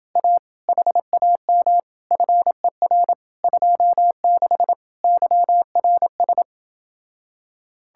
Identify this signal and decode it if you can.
Morse code, heard as a keyed tone, about 28 wpm, 700 Hz. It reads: A HAM FER 26 YRS